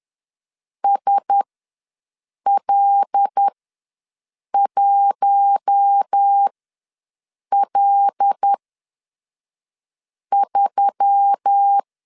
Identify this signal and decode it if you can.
Morse code, heard as a keyed tone, about 10 wpm, spaced as for 5 wpm, 800 Hz. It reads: SL1L3
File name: LEZIONI / 5 LBVC / 5 L ws 10wpm 3636